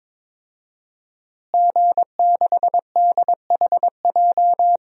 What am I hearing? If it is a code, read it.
Morse code, heard as a keyed tone, about 22 wpm, 700 Hz. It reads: G6DHJ
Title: DXCC-100 Call Signs - 5 Characters - Repeated 3x - 22wpm